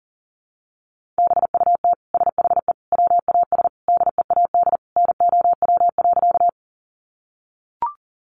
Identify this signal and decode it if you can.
Morse code, heard as a keyed tone, about 40 wpm, 700 Hz. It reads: BUT SHE WAS DEAD NOW.
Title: Sentences from Top 1500 Words 40wpm